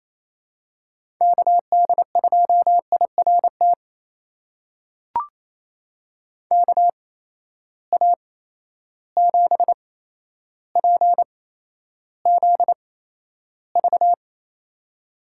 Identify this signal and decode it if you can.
Morse code, heard as a keyed tone, about 28 wpm, spaced as for 12 wpm, 700 Hz. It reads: KD2IRT KA7PZV
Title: USA Call Signs - 6 Characters - 8x Character Spacing - 28wpm